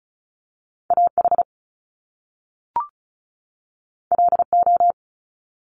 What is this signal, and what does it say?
Morse code, heard as a keyed tone, about 36 wpm, 700 Hz. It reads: AH LO
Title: Top Two Letter Combinations in Words 35wpm